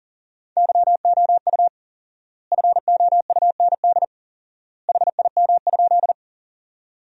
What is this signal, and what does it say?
Morse code, heard as a keyed tone, about 40 wpm, 700 Hz. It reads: YOU FOUND HIM?
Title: 4x Word-Spacing - Sentences from Top 300 Words - 40wpm